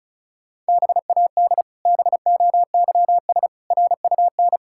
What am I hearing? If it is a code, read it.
Morse code, heard as a keyed tone, about 36 wpm, 700 Hz. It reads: BAD BOYS RUN
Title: Pangram Challenge v2 - 35wpm